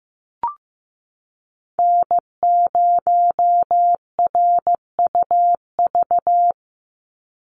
Morse code, heard as a keyed tone, about 15 wpm, 700 Hz. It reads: N0RUV
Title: USA Call Signs 15wpm